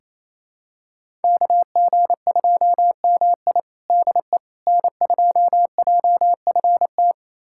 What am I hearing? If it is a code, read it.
Morse code, heard as a keyed tone, about 28 wpm, 700 Hz. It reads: KG2MI DE N2JFT
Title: USA Callsign DE Callsign 28wpm